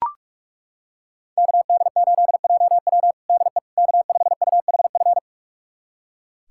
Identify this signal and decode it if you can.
Morse code, heard as a keyed tone, about 41 wpm, 700 Hz. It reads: KD8JW DE K5UHF
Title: USA Callsign DE Callsign - 5 Characters - 45wpm